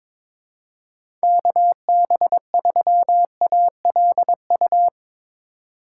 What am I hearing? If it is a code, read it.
Morse code, heard as a keyed tone, about 21 wpm, 700 Hz. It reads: KB3ALU
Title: USA Call Signs - 6 Characters - 2x Character Spacing - 22wpm